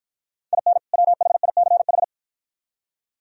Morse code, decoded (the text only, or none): IN WHICH